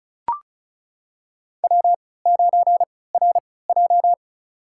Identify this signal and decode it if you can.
Morse code, heard as a keyed tone, about 36 wpm, spaced as for 20 wpm, 700 Hz. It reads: W9RJ
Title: USA Call Signs - 4 Characters - 3x Character Spacing - 35wpm